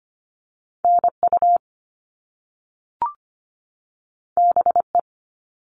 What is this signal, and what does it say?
Morse code, heard as a keyed tone, about 25 wpm, 700 Hz. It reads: NU BE
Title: All Cut Numbers - 2-Digit 25wpm